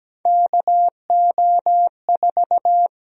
Morse code, heard as a keyed tone, about 17 wpm, 700 Hz. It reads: KO4
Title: USA Call Sign Prefixes - 17wpm